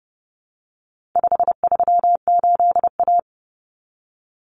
Morse code, heard as a keyed tone, about 30 wpm, 700 Hz. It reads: 538A
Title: Common Cut Numbers - 4-Digit 30wpm